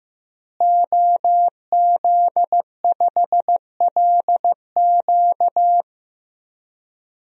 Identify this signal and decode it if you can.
Morse code, heard as a keyed tone, about 15 wpm, 700 Hz. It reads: OZ5LQ
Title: DXCC-100 Call Signs - 5 Characters - Repeated 3x - 15wpm